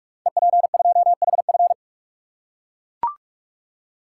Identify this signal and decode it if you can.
Morse code, heard as a keyed tone, about 44 wpm, 700 Hz. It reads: EP2HF
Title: Code Groups - 5 Letter:Numbers - 45wpm